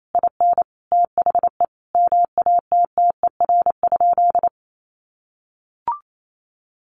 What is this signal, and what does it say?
Morse code, heard as a keyed tone, about 28 wpm, 700 Hz. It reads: IN THE MATTER?
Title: Sentences from Top 1300 Words 28wpm